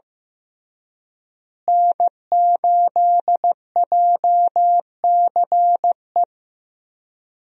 Morse code, heard as a keyed tone, about 15 wpm, 700 Hz. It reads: N8JCE
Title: USA Call Signs - 5 Characters - Repeated 3x - 15wpm